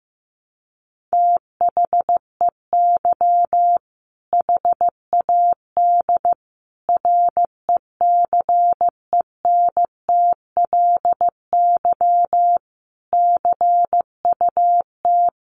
Morse code, heard as a keyed tone, about 15 wpm, 700 Hz. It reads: THEY HAD RECENTLY CUT
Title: Sentences from Top 1200 Words 15wpm